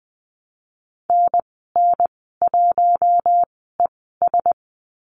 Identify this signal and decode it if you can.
Morse code, heard as a keyed tone, about 20 wpm, spaced as for 15 wpm, 700 Hz. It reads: NN1ES
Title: 042 Lesson 8g - Number 1 - Call Signs 20wpm